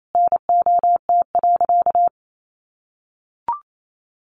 Morse code, heard as a keyed tone, about 28 wpm, 690 Hz. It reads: NOT.
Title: Sentences from Top 900 Words 28wpm